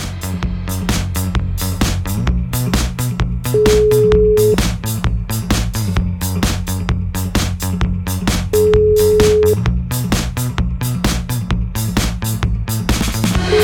Ringback tone (Europe, 425 Hz).